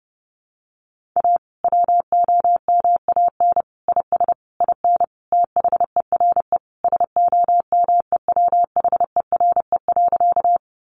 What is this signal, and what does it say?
Morse code, heard as a keyed tone, about 30 wpm, 700 Hz. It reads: A WOMAN IS IN THERE SOMEWHERE.
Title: Sentences from Top 1400 Words 30wpm